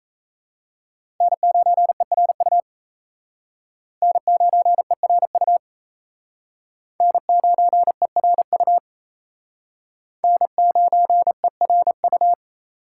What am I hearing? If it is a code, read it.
Morse code, heard as a keyed tone, about 34 wpm, 700 Hz. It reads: N9ERU N9ERU N9ERU N9ERU
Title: Speed Racing - USA 1x3 Call Signs 28wpm